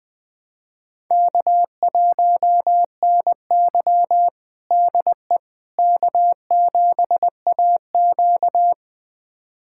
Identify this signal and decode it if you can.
Morse code, heard as a keyed tone, about 20 wpm, 700 Hz. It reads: K1NY DE K7AQ
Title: USA Callsign DE Callsign - 4 Characters - 20wpm